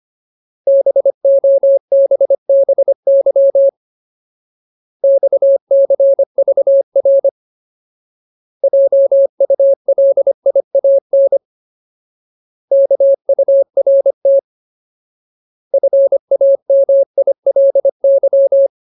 Morse code, heard as a keyed tone, about 25 wpm, 550 Hz. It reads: BOBBY XCVR JULIAN KURT FAMILY
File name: Ultimate Word List / 25WPM Ultimate